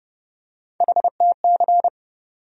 Morse code, heard as a keyed tone, about 30 wpm, 700 Hz. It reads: HTC
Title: USA Call Sign Postfixes - 3 Letters 30wpm